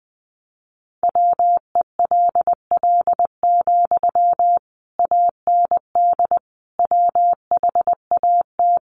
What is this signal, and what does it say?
Morse code, heard as a keyed tone, about 20 wpm, 700 Hz. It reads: WELL, AND WHAT
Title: Sentences from Top 200 Words 20wpm